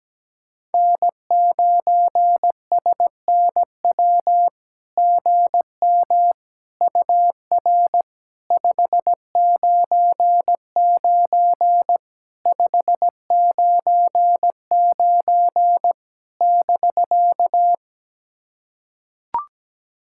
Morse code, heard as a keyed tone, about 17 wpm, 700 Hz. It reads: N9SNW GM UR 599 599 <BK>
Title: SOTA QSO Practice 17wpm